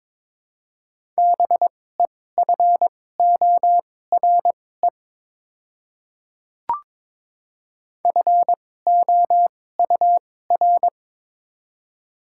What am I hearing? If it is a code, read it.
Morse code, heard as a keyed tone, about 22 wpm, spaced as for 16 wpm, 700 Hz. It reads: BEFORE FOUR
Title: 2x Character Spacing - Top 500 Words - 22wpm